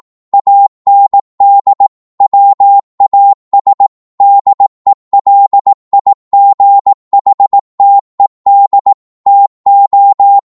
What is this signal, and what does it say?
Morse code, heard as a keyed tone, about 18 wpm, 800 Hz. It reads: AND WAS DELIGHTED TO
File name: 18wpm